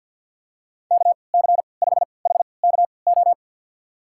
Morse code, heard as a keyed tone, about 51 wpm, spaced as for 28 wpm, 700 Hz. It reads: KC5HKO
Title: USA Call Signs - 6 Characters - 3x Character Spacing - 50wpm